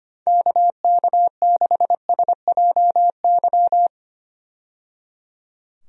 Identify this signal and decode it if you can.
Morse code, heard as a keyed tone, about 25 wpm, 700 Hz. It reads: KK6SJY
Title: USA Call Signs - 6 Characters - 25wpm